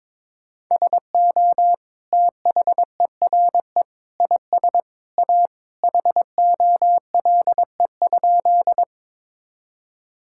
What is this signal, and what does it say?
Morse code, heard as a keyed tone, about 22 wpm, 700 Hz. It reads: SO THERE IS A HOLE?
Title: Sentences from Top 1600 Words 22wpm